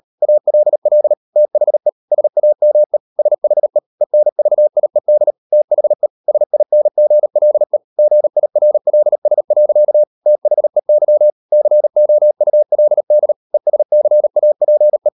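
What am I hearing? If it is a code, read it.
Morse code, heard as a keyed tone, about 38 wpm, 600 Hz. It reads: ALL THE SAME SHE ENVIED THE SINGLE GIRLS. THEY COULD ESCAPE
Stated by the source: Morse code practice at 38 WPM, sending Ken Follett's novel Never in English, from CW med Gnister.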